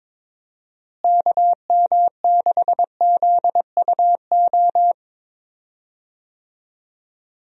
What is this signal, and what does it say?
Morse code, heard as a keyed tone, about 22 wpm, 700 Hz. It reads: KM6ZUO
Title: USA Call Signs - 6 Characters - Repeated 2x - 22wpm